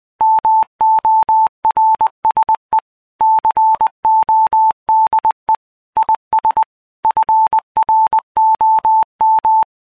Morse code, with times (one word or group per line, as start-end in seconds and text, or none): 0.2-2.8 MORSE
3.2-5.5 CODE
6.0-6.6 IS
7.0-9.6 FROM